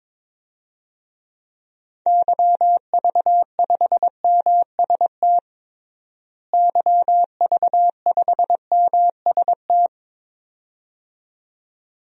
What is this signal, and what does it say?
Morse code, heard as a keyed tone, about 22 wpm, 700 Hz. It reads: YV5MST YV5MST
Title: DXCC-100 Call Signs - 6 Characters - Repeated 2x - 22wpm